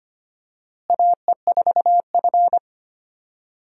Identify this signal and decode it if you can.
Morse code, heard as a keyed tone, about 25 wpm, 700 Hz. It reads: AE4F
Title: USA Call Signs - 4 Characters - Repeated 3x - 25wpm